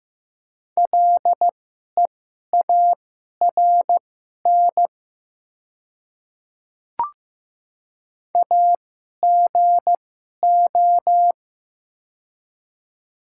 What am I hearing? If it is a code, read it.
Morse code, heard as a keyed tone, about 15 wpm, spaced as for 11 wpm, 700 Hz. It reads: LEARN AGO
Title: 2x Character Spacing - Top 500 Words - 15wpm